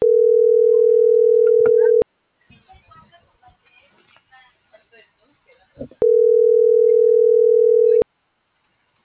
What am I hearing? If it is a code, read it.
Ringback tone (North America), 440 + 480 Hz.